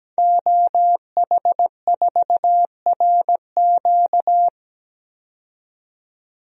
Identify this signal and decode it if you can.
Morse code, heard as a keyed tone, about 17 wpm, 700 Hz. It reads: OH4RQ
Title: DXCC-100 Call Signs - 5 Characters - 17wpm